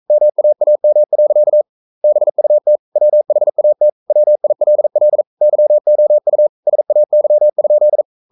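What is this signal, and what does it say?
Morse code, heard as a keyed tone, about 42 wpm, 600 Hz. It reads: MAAM. BUT WHAT WILL YOU SAY?
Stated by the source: Morse code practice at 41 wpm, from CW med Gnister.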